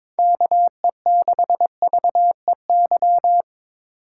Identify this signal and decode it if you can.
Morse code, heard as a keyed tone, about 22 wpm, 700 Hz. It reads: KE6VEY
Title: USA Call Signs - 2x Character Spacing - 22wpm